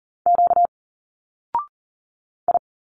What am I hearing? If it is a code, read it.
Morse code, heard as a keyed tone, about 39 wpm, 700 Hz. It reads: Q I
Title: Single Letters 40wpm